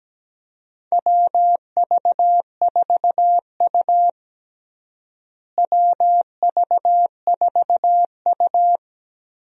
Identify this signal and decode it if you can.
Morse code, heard as a keyed tone, about 17 wpm, 700 Hz. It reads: WV4U WV4U
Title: USA Call Signs - 4 Characters - Repeated 3x - 17wpm